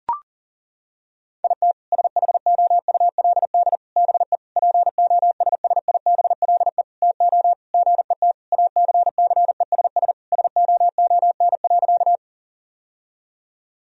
Morse code, heard as a keyed tone, about 40 wpm, 700 Hz. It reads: IT SHOULD BE POSSIBLE TO GET ACCESS SOON.
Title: Sentences from Top 1000 Words - Review Part 4 - 40wpm